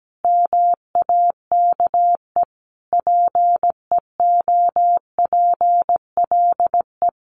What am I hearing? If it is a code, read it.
Morse code, heard as a keyed tone, about 17 wpm, 700 Hz. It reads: MAKE PEOPLE
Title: Easy Phrases 17wpm